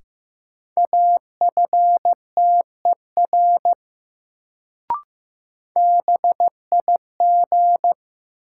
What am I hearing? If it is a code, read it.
Morse code, heard as a keyed tone, about 15 wpm, 700 Hz. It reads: AFTER BIG
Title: Sentences from Top 200 Words - New Word Drill - 15wpm